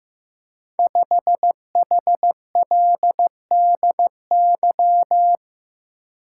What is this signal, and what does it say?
Morse code, heard as a keyed tone, about 16 wpm, 700 Hz. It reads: 5HLDY
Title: Code Groups - 5 Letter:Numbers - 15wpm